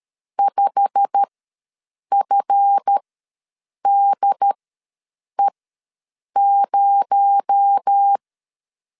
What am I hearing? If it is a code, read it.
Morse code, heard as a keyed tone, about 13 wpm, spaced as for 7 wpm, 800 Hz. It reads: 5FDE0
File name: LEZIONI / 8 YJ45 / 8 YJ45 ws 12wpm 3146